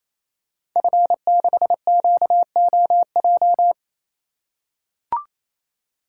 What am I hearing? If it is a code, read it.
Morse code, heard as a keyed tone, about 29 wpm, 700 Hz. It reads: F6QOJ
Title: Code Groups - 5 Letter:Numbers - 28wpm